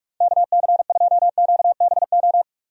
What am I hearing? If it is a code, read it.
Morse code, heard as a keyed tone, about 45 wpm, 700 Hz. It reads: KC2QBO
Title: USA Call Signs - 6 Characters - 45wpm